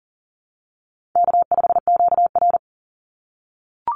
Morse code, heard as a keyed tone, about 40 wpm, 700 Hz. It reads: K5QR
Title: NA SOTA Chaser Callsigns 40wpm